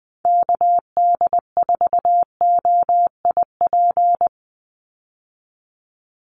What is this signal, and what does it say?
Morse code, heard as a keyed tone, about 20 wpm, 700 Hz. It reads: KD4OIP
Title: USA Call Signs 20wpm